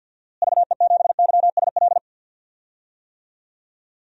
Morse code, heard as a keyed tone, about 47 wpm, 700 Hz. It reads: VE7YSL
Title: DXCC-100 Call Signs - 6 Characters - Repeated 2x - 50wpm